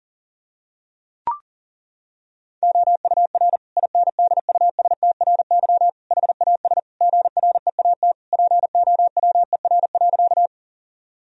Morse code, heard as a keyed tone, about 37 wpm, 700 Hz. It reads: OUR INDUSTRY HAS GREAT POWER.
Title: Sentences from Top 1000 Words - Review Part 4 - 40wpm